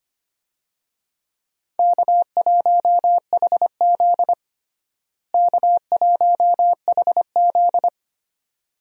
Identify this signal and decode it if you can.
Morse code, heard as a keyed tone, about 25 wpm, 700 Hz. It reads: K1HZ K1HZ
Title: USA Call Signs - 4 Characters - Repeated 2x - 25wpm